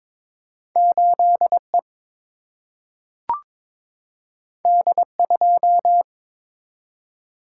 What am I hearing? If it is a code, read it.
Morse code, heard as a keyed tone, about 22 wpm, 700 Hz. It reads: 8E D2
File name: practice / 22wpm / Two Letter-Number Combinations - 22wpm